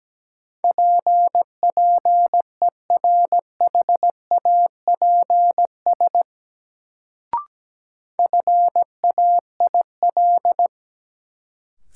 Morse code, heard as a keyed tone, about 17 wpm, 700 Hz. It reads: PPERHAPS FAIL
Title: Sentences from Top 1100 Words - New Word Drill Alt - 17wpm